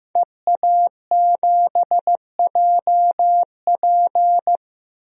Morse code, heard as a keyed tone, about 15 wpm, 700 Hz. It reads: EA7JP
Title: DXCC-100 Call Signs - 5 Characters - 15wpm